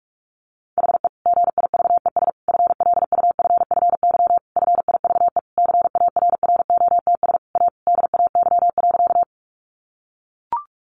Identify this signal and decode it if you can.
Morse code, heard as a keyed tone, about 44 wpm, 700 Hz. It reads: HE GIVES FLUFFY FIVE CARROTS A DAY.
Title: Easy Phrases 45wpm